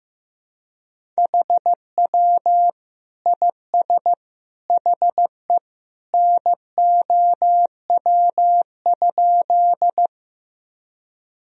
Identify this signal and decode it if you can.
Morse code, heard as a keyed tone, about 15 wpm, 700 Hz. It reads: HW IS HE NOW?
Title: Sentences from Top 100 Words and Abbreviations 15wpm